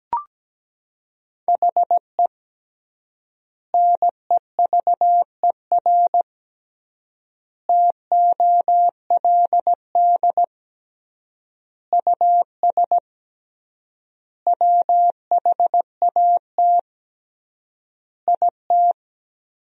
Morse code, heard as a keyed tone, about 17 wpm, 700 Hz. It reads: HE NEVER TOLD US WHAT IT
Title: 3x Word-Spacing - Sentences from Top 400 Words - 17wpm